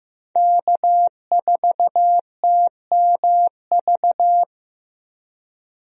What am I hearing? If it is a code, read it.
Morse code, heard as a keyed tone, about 15 wpm, 700 Hz. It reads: K4TMV
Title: USA Call Signs - 5 Characters - 15wpm